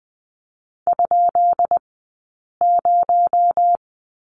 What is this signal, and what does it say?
Morse code, heard as a keyed tone, about 20 wpm, spaced as for 8 wpm, 700 Hz. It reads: ?0
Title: 217 Lesson 35d - Number 0 - Two Characters 20wpm